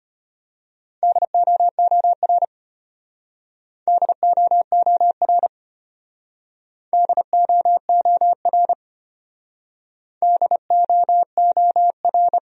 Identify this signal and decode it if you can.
Morse code, heard as a keyed tone, about 34 wpm, 700 Hz. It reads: DOOR DOOR DOOR DOOR